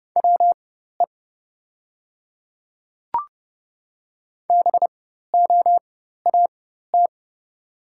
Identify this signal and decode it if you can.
Morse code, heard as a keyed tone, about 29 wpm, spaced as for 14 wpm, 700 Hz. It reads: WE BOAT
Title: 4x Character Spacing - Top 500 Words - 30wpm